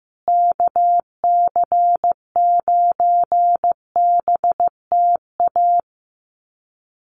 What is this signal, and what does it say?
Morse code, heard as a keyed tone, about 15 wpm, 700 Hz. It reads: KC9BTA
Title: USA Call Signs Repeated 2x 15wpm